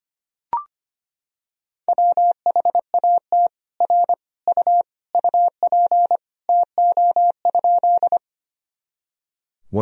Morse code, heard as a keyed tone, about 25 wpm, 700 Hz. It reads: WHAT R U UP TO?
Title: Sentences from Top 100 Words and Abbreviations 25wpm